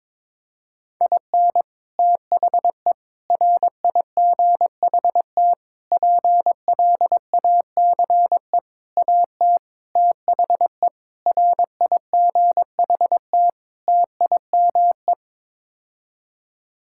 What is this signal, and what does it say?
Morse code, heard as a keyed tone, about 22 wpm, 700 Hz. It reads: IN THE RIGHT PLACE AT THE RIGHT TIME
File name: practice / 22wpm / Sets of 8 Words - 22wpm